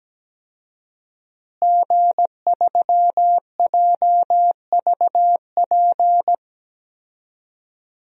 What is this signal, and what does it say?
Morse code, heard as a keyed tone, about 17 wpm, 700 Hz. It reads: G3JVP
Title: DXCC-100 Call Signs - 5 Characters - Repeated 2x - 17wpm